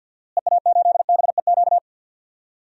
Morse code, heard as a keyed tone, about 53 wpm, 700 Hz. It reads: EA8BEQ
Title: DXCC-100 Call Signs - 6 Characters - Repeated 2x - 50wpm